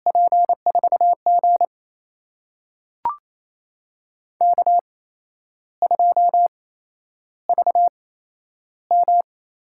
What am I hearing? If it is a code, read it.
Morse code, heard as a keyed tone, about 28 wpm, spaced as for 11 wpm, 700 Hz. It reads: P4G K2VM
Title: USA Call Signs - 4 Characters - 8x Character Spacing - 28wpm